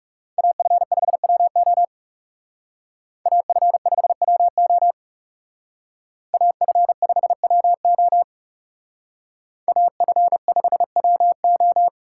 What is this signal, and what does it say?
Morse code, heard as a keyed tone, about 39 wpm, 700 Hz. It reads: AF5WO AF5WO AF5WO AF5WO